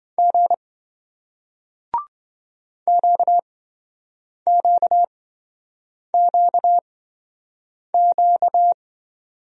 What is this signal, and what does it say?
Morse code, heard as a keyed tone, about 28 wpm, spaced as for 7 wpm, 700 Hz. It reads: G QQQQ